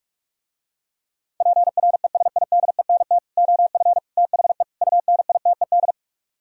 Morse code, heard as a keyed tone, about 49 wpm, 700 Hz. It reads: PRESIDENT OF THE UNITED